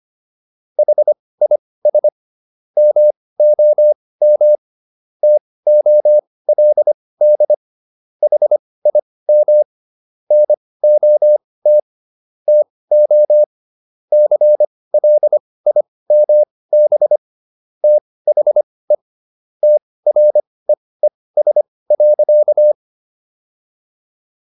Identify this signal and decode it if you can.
Morse code, heard as a keyed tone, about 25 wpm, spaced as for 18 wpm, 600 Hz. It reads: HIS MOM TOLD HIM NOT TO CLIMB THE TREES.